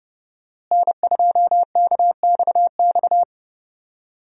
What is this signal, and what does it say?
Morse code, heard as a keyed tone, about 31 wpm, 700 Hz. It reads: N2KXX
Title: USA Call Signs - 5 Characters - Repeated 3x - 30wpm